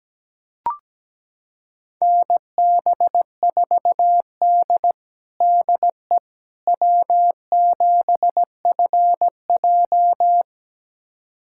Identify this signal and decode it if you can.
Morse code, heard as a keyed tone, about 17 wpm, 700 Hz. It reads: NB4D DE W7FJ
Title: USA Callsign DE Callsign - 4 Characters - 17wpm